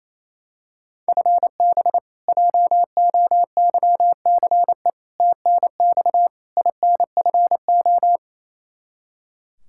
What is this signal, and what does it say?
Morse code, heard as a keyed tone, about 28 wpm, 700 Hz. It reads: FB JOYCE TNX INFO
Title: K1USN SST QSO Practice 28wpm